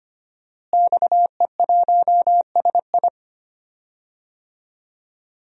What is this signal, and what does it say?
Morse code, heard as a keyed tone, about 25 wpm, 700 Hz. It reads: XE1SI
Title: DXCC-100 Call Signs - 5 Characters - Repeated 3x - 25wpm